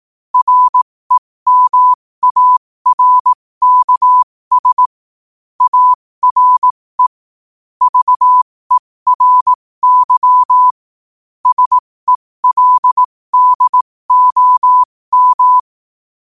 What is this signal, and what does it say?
Morse code, heard as a keyed tone, about 18 wpm, spaced as for 15 wpm, 1000 Hz. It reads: REMARKS ARE VERY SELDOM
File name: 15wpm